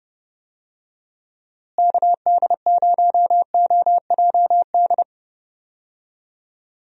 Morse code, heard as a keyed tone, about 29 wpm, 700 Hz. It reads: KD0OJD
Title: USA Call Signs - 6 Characters - Repeated 2x - 30wpm